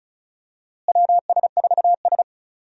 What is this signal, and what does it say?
Morse code, heard as a keyed tone, about 36 wpm, 700 Hz. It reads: WS4S
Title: USA Call Signs - 4 Characters - Repeated 2x - 35wpm